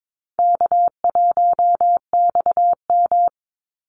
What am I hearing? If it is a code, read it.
Morse code, heard as a keyed tone, about 22 wpm, 700 Hz. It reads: K1XM
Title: Call Signs Repeated 3x - 22wpm